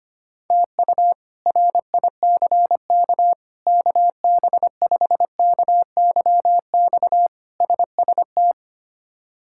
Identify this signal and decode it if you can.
Morse code, heard as a keyed tone, about 25 wpm, 700 Hz. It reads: TU RICK KB5KYX SST
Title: K1USN SST QSO Practice - 25wpm